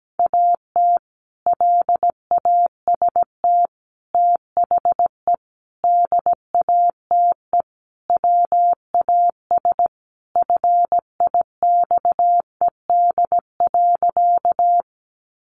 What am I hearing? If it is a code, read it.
Morse code, heard as a keyed tone, about 17 wpm, 700 Hz. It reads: AT LAST THE DATE WAS FIXED.
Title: Sentences from Top 1100 Words 17wpm